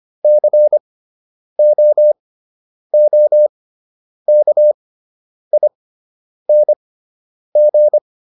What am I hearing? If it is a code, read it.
Morse code, heard as a keyed tone, about 25 wpm, spaced as for 9 wpm, 600 Hz. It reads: COOKING